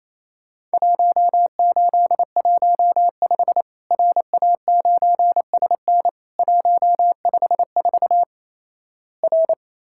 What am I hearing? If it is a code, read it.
Morse code, heard as a keyed tone, about 28 wpm, 700 Hz. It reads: 1815 RA9SN 154 R
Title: Worked All Europe DX Contest - QTC Exchange - 28wpm